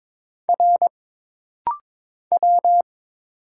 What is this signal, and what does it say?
Morse code, heard as a keyed tone, about 22 wpm, 700 Hz. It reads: R W
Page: This rapid-fire timing is geared to push your ICR (Instant Character Recognition) skills.